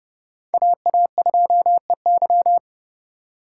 Morse code, heard as a keyed tone, about 28 wpm, 700 Hz. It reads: AA2EY